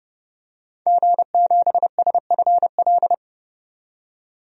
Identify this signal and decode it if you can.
Morse code, heard as a keyed tone, about 32 wpm, 700 Hz. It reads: G7SFL
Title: DXCC-100 Call Signs - 5 Characters - 30wpm